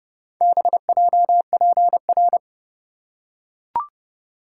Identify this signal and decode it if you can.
Morse code, heard as a keyed tone, about 29 wpm, 700 Hz. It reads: BJPR